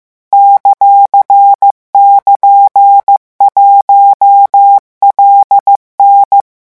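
Morse code, heard as a keyed tone, about 15 wpm, 800 Hz. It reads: ;<KN>1LN
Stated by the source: Morse code revision lesson 16 at 14 words per minute.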